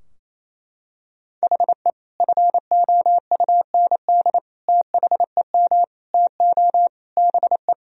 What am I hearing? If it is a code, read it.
Morse code, heard as a keyed tone, about 27 wpm, 700 Hz. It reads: HE FOUND THEM TO BE
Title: Sentences from Top 1000 Words - Review Part 3 - 28wpm